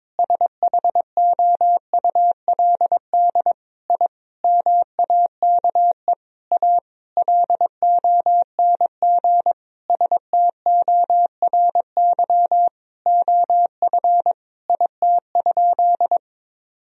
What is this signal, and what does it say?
Morse code, heard as a keyed tone, about 22 wpm, 700 Hz. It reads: SHOULD I MAKE A LONG STORY OF IT?
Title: Sentences from Top 1000 Words - Review Part 1 - 22wpm